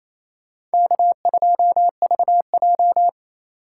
Morse code, heard as a keyed tone, about 35 wpm, 700 Hz. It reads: K2VJ
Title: Speed Racing - USA 1x2 Call Signs 28wpm